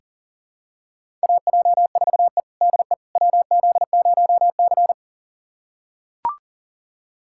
Morse code, heard as a keyed tone, about 40 wpm, 700 Hz. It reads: AJ4E DE WZ0C
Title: USA Callsign DE Callsign - 4 Characters - 40wpm